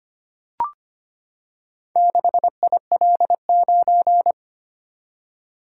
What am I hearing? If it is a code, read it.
Morse code, heard as a keyed tone, about 25 wpm, 700 Hz. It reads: 6IL9
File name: practice / 25wpm / Four Letter-Number Combinations - 25wpm